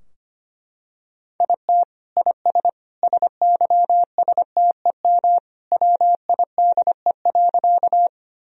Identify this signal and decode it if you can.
Morse code, heard as a keyed tone, about 25 wpm, 700 Hz. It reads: IT IS SYSTEM WIDE.
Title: Sentences from Top 1000 Words - Review Part 3 - 25wpm